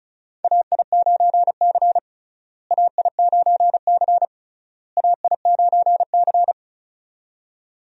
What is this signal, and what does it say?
Morse code, heard as a keyed tone, about 34 wpm, 700 Hz. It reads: AI9C AI9C AI9C
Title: USA Call Signs - 4 Characters - Repeated 3x - 35wpm